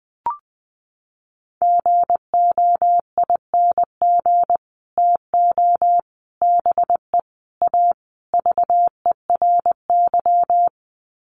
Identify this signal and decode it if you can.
Morse code, heard as a keyed tone, about 20 wpm, 700 Hz. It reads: GOING TO BE A VERY
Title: Sets of 5 Words Encore 20wpm